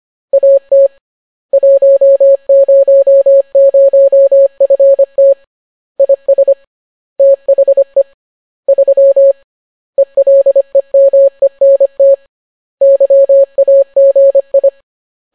Morse code, heard as a keyed tone, about 25 wpm, 550 Hz. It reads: AT 100FT IS THE 3 ELEMENT YAGI